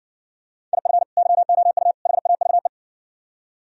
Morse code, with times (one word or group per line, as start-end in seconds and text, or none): 0.7-1.0 IF
1.2-1.9 YOU
2.1-2.7 HAVE